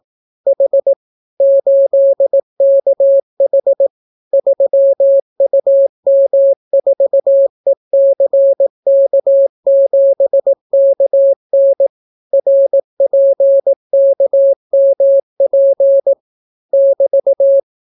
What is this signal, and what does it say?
Morse code, heard as a keyed tone, about 18 wpm, 550 Hz. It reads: H 8KH 3UM4ECK7KN RPKMP <BT>